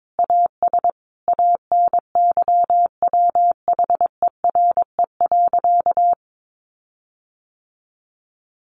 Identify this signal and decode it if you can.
Morse code, heard as a keyed tone, about 22 wpm, 700 Hz. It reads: AS ANYWHERE.
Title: Sentences from Top 1400 Words 22wpm